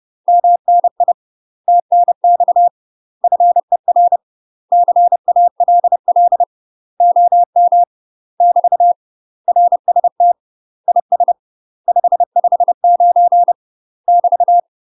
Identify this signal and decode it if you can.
Morse code, heard as a keyed tone, about 30 wpm, 700 Hz. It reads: MNI TNX FER CALL OM <BT> RST IS 559 <BT>
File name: UK MASTER QSOs TRAINING / 30WPM / UK MASTER QSOs 0044